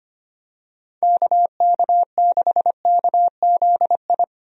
Morse code, heard as a keyed tone, about 25 wpm, 700 Hz. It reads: KK6KZI
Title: USA Call Signs - 6 Characters - 25wpm